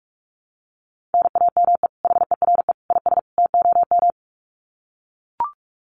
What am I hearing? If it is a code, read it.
Morse code, heard as a keyed tone, about 44 wpm, 700 Hz. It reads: NAME HERE IS TOM
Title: Common QSO Phrases 45wpm